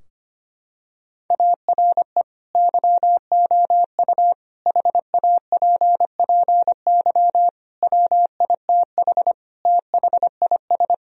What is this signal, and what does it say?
Morse code, heard as a keyed tone, about 25 wpm, 700 Hz. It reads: ARE YOU HAPPY WITH THIS
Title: Sentences from Top 1000 Words - Review Part 2 - 25wpm